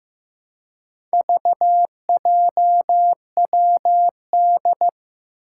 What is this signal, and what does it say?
Morse code, heard as a keyed tone, about 15 wpm, 700 Hz. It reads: VJWD
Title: Four Challenging Letter-Number Combinations 15wpm